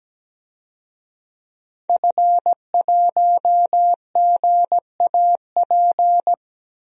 Morse code, heard as a keyed tone, about 17 wpm, 700 Hz. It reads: F1GAP
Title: DXCC-100 Call Signs - 5 Characters - Repeated 3x - 17wpm